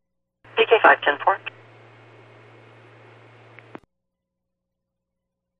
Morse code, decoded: none

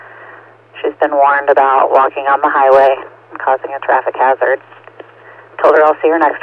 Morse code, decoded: none

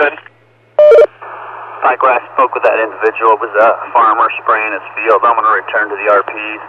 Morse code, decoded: none

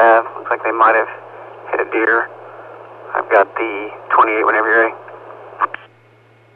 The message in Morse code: none

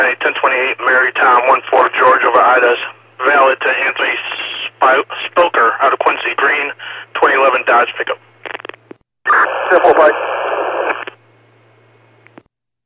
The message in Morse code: none

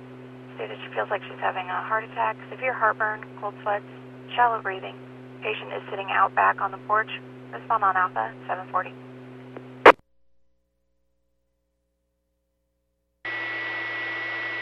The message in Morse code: none

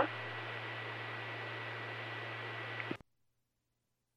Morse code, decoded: none

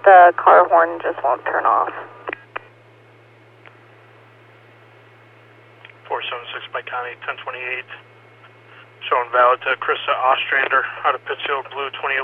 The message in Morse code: none

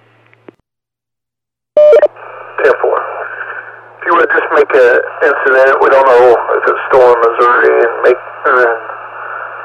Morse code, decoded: none

